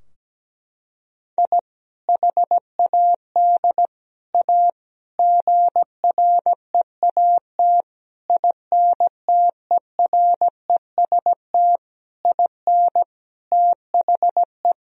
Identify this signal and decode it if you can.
Morse code, heard as a keyed tone, about 17 wpm, 700 Hz. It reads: I HAD A GREAT INTEREST IN THE
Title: Sentences from Top 1000 Words - Review Part 2 - 17wpm